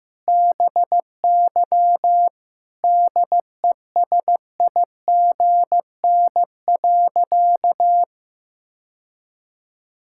Morse code, heard as a keyed tone, about 15 wpm, 700 Hz. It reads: BY DESIGN.
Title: Sentences from Top 1000 Words - Review Part 5 - 15wpm